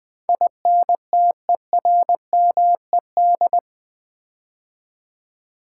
Morse code, heard as a keyed tone, about 20 wpm, 700 Hz. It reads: INTERMED
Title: Build A Word - 4 Letters At A Time - 12-Letter Words - 20wpm